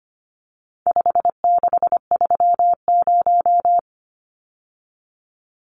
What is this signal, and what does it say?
Morse code, heard as a keyed tone, about 25 wpm, 700 Hz. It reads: 5630
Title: Numbers - 4-Digit 25wpm